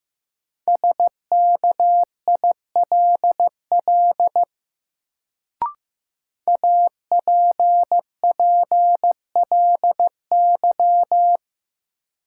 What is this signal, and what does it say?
Morse code, heard as a keyed tone, about 15 wpm, 700 Hz. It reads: SKILL APPLY
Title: Sentences from Top 1200 Words - New Word Drill - 15wpm